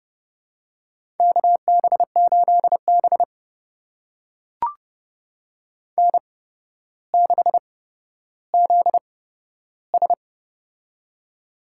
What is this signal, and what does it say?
Morse code, heard as a keyed tone, about 30 wpm, spaced as for 13 wpm, 700 Hz. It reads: KB8B N6ZS
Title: USA Call Signs - 4 Characters - 8x Character Spacing - 30wpm